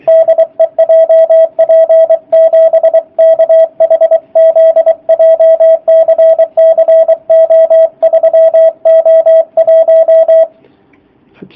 Morse code, heard as a keyed tone, about 24 wpm, 650 Hz, with radio noise. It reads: DE JP7KHZ JCC O3O1
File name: jh8cbh / 2019cw / sp27- w